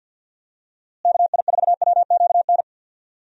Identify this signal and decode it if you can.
Morse code, heard as a keyed tone, about 51 wpm, 700 Hz. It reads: KI4WQN
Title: USA Call Signs - 6 Characters - Repeated 2x - 50wpm